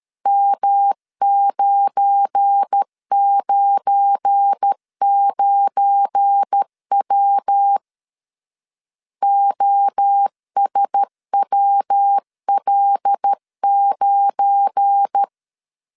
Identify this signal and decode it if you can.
Morse code, heard as a keyed tone, about 13 wpm, 800 Hz. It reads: M999W OSWL9
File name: LEZIONI / 7 GF9Z / 7 GF9 12wpm 2319